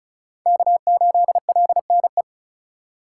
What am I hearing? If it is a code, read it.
Morse code, heard as a keyed tone, about 36 wpm, 700 Hz. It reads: K8LNE
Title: USA Call Signs - 3x Character Spacing - 35wpm